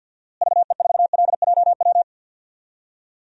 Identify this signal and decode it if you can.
Morse code, heard as a keyed tone, about 52 wpm, 700 Hz. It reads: VE4LJW
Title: DXCC-100 Call Signs - 6 Characters - Repeated 2x - 50wpm